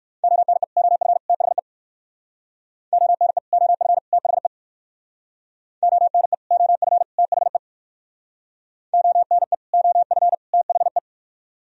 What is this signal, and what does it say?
Morse code, heard as a keyed tone, about 62 wpm, 700 Hz. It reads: ONE OF THE ONE OF THE ONE OF THE ONE OF THE